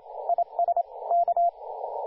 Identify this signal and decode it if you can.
Morse code, heard as a keyed tone, about 29 wpm, 690 Hz, with radio noise. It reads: IS K
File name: du1-ja1pbv-1-28-040mhz-cw-cqing